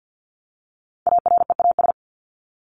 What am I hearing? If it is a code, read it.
Morse code, heard as a keyed tone, about 48 wpm, 700 Hz. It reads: AREAS
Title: ES:S Suffix - Top 100 Words with Base Words 50wpm